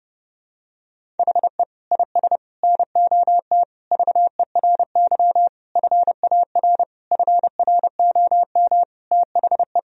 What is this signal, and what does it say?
Morse code, heard as a keyed tone, about 30 wpm, 700 Hz. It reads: HE IS NOT VERY FAR FROM THE
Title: Sentences from Top 1000 Words - Review Part 1 - 30wpm